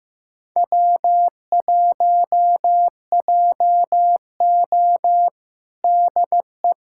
Morse code, heard as a keyed tone, about 15 wpm, 700 Hz. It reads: W1JO DE
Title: USA Callsign DE Callsign - 4 Characters - 15wpm